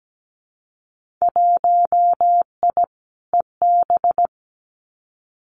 Morse code, heard as a keyed tone, about 17 wpm, 700 Hz. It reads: 1I EB
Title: Winter Field Day Exchanges Encore 17wpm